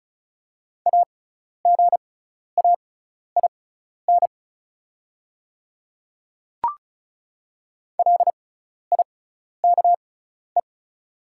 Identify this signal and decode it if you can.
Morse code, heard as a keyed tone, about 36 wpm, spaced as for 12 wpm, 700 Hz. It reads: AGAIN LIKE
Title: 6x Character Spacing - Top 200 Words - 35wpm